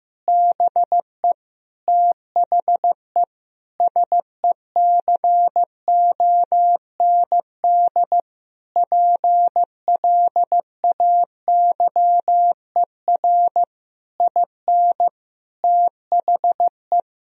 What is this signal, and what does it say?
Morse code, heard as a keyed tone, about 15 wpm, 700 Hz. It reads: BE THE SECOND PLAYER IN THE
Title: Sentences from Top 1000 Words - Review Part 4 - 15wpm